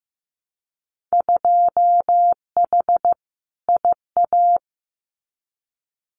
Morse code, heard as a keyed tone, about 15 wpm, 700 Hz. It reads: 2H IA
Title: Winter Field Day Exchanges Encore 15wpm